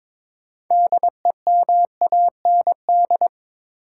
Morse code, heard as a keyed tone, about 22 wpm, 700 Hz. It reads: DEMAND
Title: Sentences from Top 800 Words - New Word Drill - 22wpm